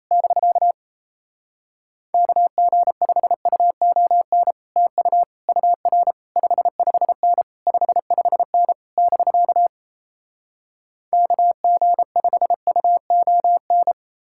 Morse code, heard as a keyed tone, about 37 wpm, 700 Hz. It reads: <BK> KG5UON TU UR 55N 55N <BK> KG5UON